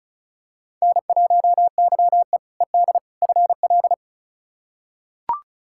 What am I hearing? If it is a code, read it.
Morse code, heard as a keyed tone, about 36 wpm, 700 Hz. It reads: N1YE ED FL
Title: K1USN SST QSO Practice - 35wpm